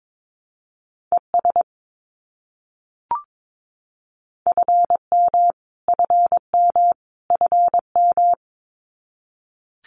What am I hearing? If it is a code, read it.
Morse code, heard as a keyed tone, about 22 wpm, 700 Hz. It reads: ES FM FM FM
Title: Abbreviations Repeated 3x - 22wpm